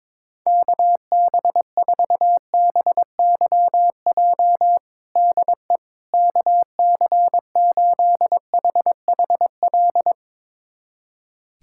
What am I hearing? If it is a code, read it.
Morse code, heard as a keyed tone, about 22 wpm, 700 Hz. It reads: KB4BYJ DE KC8HHL
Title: USA Callsign DE Callsign 22wpm